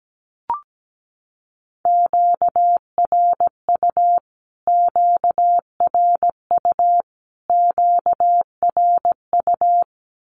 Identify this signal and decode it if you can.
Morse code, heard as a keyed tone, about 17 wpm, 700 Hz. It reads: QRU QRU QRU
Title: Q-Codes Repeated 3x 17wpm